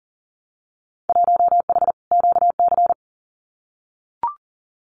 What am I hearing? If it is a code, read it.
Morse code, heard as a keyed tone, about 40 wpm, 700 Hz. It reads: 1H QC
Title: Winter Field Day Exchanges 40wpm